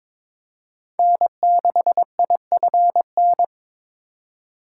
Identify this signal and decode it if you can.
Morse code, heard as a keyed tone, about 22 wpm, 700 Hz. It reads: N6IFN
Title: USA Call Signs - 5 Characters - 4x Character Spacing - 22wpm